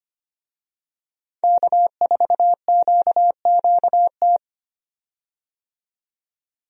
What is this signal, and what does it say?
Morse code, heard as a keyed tone, about 25 wpm, 700 Hz. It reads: K4QQT
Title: USA Call Signs - 5 Characters - 25wpm